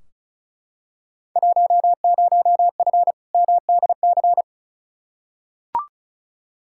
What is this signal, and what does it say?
Morse code, heard as a keyed tone, about 36 wpm, 700 Hz. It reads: 10F MDC
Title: Field Day Exchanges Encore - 35wpm